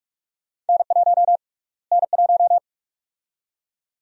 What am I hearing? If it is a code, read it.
Morse code, heard as a keyed tone, about 46 wpm, 700 Hz. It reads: N1 N1
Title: USA Call Sign Prefixes - Repeated 2x - 45wpm